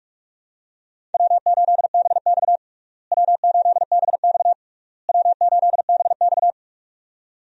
Morse code, heard as a keyed tone, about 41 wpm, 700 Hz. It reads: W8BX W8BX W8BX
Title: USA Call Signs - 4 Characters - Repeated 3x - 45wpm